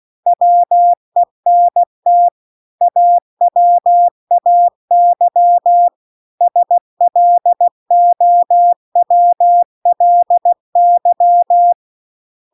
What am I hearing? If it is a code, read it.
Morse code, heard as a keyed tone, about 16 wpm, 700 Hz. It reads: WENT AWAY SLOWLY